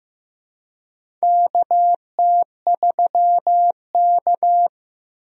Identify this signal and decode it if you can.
Morse code, heard as a keyed tone, about 15 wpm, 700 Hz. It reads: KT3K
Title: USA Call Signs - 4 Characters - Repeated 2x - 15wpm